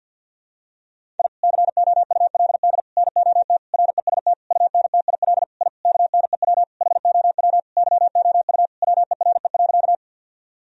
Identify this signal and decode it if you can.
Morse code, heard as a keyed tone, about 46 wpm, 700 Hz. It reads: I COULD NOT REST UNTIL I KNEW HOW YOU WERE.
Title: Sentences from Top 1000 Words - Review Part 1 - 50wpm